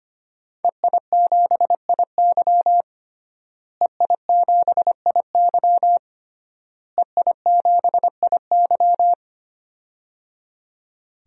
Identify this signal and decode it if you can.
Morse code, heard as a keyed tone, about 25 wpm, 700 Hz. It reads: EI7IY EI7IY EI7IY
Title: DXCC-100 Call Signs - 5 Characters - Repeated 3x - 25wpm